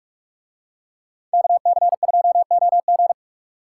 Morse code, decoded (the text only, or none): KC2OG